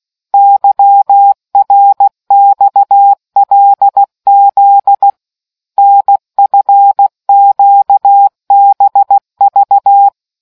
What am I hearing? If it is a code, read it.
Morse code, heard as a keyed tone, about 16 wpm, 800 Hz. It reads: YRXLZ NFQBV